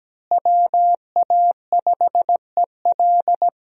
Morse code, heard as a keyed tone, about 17 wpm, 700 Hz. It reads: WA5EL
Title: USA Call Signs - 5 Characters - Repeated 3x - 17wpm